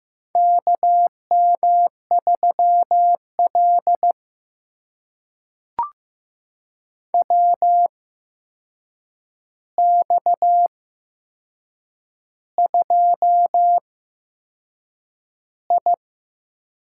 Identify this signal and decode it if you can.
Morse code, heard as a keyed tone, about 15 wpm, spaced as for 6 wpm, 700 Hz. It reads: KM3L WX2I
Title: USA Call Signs - 4 Characters - 8x Character Spacing - 15wpm